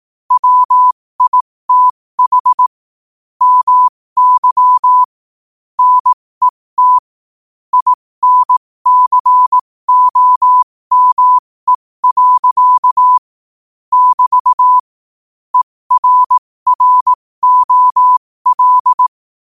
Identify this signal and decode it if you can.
Morse code, heard as a keyed tone, about 18 wpm, spaced as for 15 wpm, 1000 Hz. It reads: WITH MY NET INCOME. <BT> ERROL
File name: qotdfiles / 15wpm